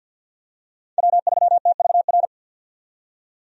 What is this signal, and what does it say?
Morse code, heard as a keyed tone, about 52 wpm, 700 Hz. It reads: W3TVR
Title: USA Call Signs - 5 Characters - 50wpm